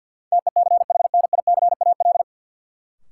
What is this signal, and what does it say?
Morse code, heard as a keyed tone, about 48 wpm, 700 Hz. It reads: TECHNICAL
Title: Sentences from Top 1100 Words - New Word Drill - 50wpm